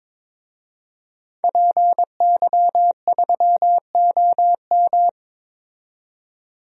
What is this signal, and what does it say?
Morse code, heard as a keyed tone, about 22 wpm, 700 Hz. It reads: PY3OM
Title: DXCC-100 Call Signs - 5 Characters - 22wpm